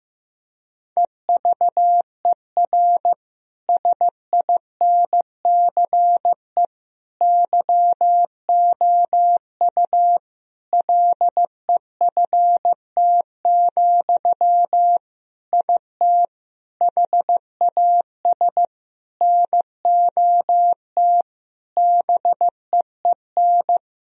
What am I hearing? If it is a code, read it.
Morse code, heard as a keyed tone, about 15 wpm, 700 Hz. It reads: EVER SINCE YOU LEFT, IT HAS NOT BEEN
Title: Sentences from Top 1000 Words - Review Part 1 - 15wpm